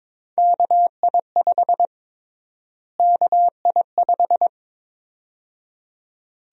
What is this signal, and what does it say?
Morse code, heard as a keyed tone, about 22 wpm, 700 Hz. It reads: KI5 KI5
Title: USA Call Sign Prefixes - Repeated 2x - 22wpm